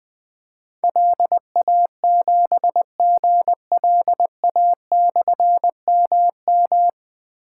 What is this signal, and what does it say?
Morse code, heard as a keyed tone, about 20 wpm, 700 Hz. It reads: LA7GLA/MM
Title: Difficult Call Signs 20wpm